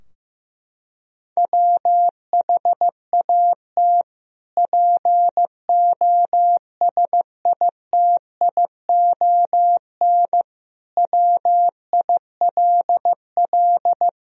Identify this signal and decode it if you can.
Morse code, heard as a keyed tone, about 15 wpm, 700 Hz. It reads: WHAT POSITION WILL
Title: Sentences from Top 1000 Words - Review Part 3 - 15wpm